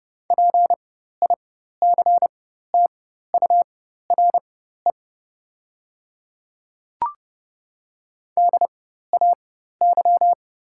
Morse code, heard as a keyed tone, about 30 wpm, spaced as for 14 wpm, 700 Hz. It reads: PICTURE DAY
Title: 4x Character Spacing - Top 200 Words - 30wpm